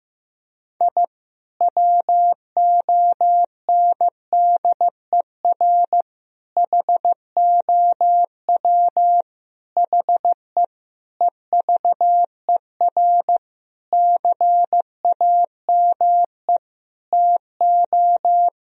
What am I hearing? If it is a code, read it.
Morse code, heard as a keyed tone, about 15 wpm, 700 Hz. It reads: I WONDER HOW HE EVER CAME TO
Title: Sentences from Top 1000 Words - Review Part 3 - 15wpm